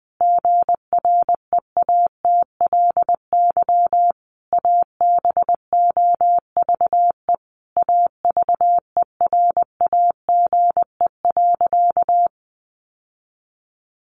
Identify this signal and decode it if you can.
Morse code, heard as a keyed tone, about 20 wpm, 700 Hz. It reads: GREATLY ABOVE AVERAGE.
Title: Sentences from Top 1400 Words 20wpm